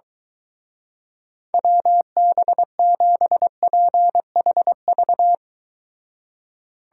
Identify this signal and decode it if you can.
Morse code, heard as a keyed tone, about 23 wpm, 700 Hz. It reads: WB7PHV